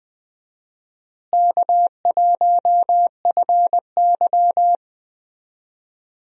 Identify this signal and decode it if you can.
Morse code, heard as a keyed tone, about 20 wpm, 700 Hz. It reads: K1FY
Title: USA Call Signs - 4 Characters - Repeated 2x - 20wpm